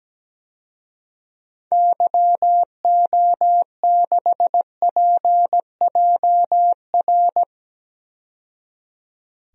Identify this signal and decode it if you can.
Morse code, heard as a keyed tone, about 17 wpm, 700 Hz. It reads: YO6PJR
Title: DXCC-100 Call Signs - 6 Characters - 17wpm